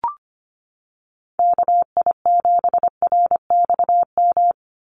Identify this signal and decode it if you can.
Morse code, heard as a keyed tone, about 25 wpm, 700 Hz. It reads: KI7RXM
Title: USA Call Signs 25wpm